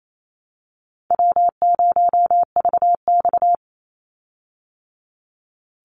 Morse code, heard as a keyed tone, about 28 wpm, 700 Hz. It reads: W0VX
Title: Call Signs Repeated 2x - 28wpm